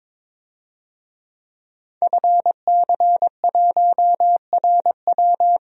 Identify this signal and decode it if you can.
Morse code, heard as a keyed tone, about 23 wpm, 700 Hz. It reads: FC1RW
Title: DXCC-100 Call Signs - 5 Characters - Repeated 3x - 22wpm